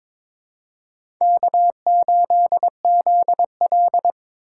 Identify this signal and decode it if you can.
Morse code, heard as a keyed tone, about 22 wpm, 700 Hz. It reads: K8ZL
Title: USA Call Signs - 3x Character Spacing - 22wpm